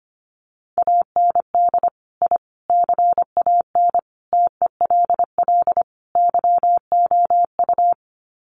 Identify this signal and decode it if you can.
Morse code, heard as a keyed tone, about 25 wpm, 700 Hz. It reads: AND I CAN TELL YOU
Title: Sets of 5 Words Encore2 25wpm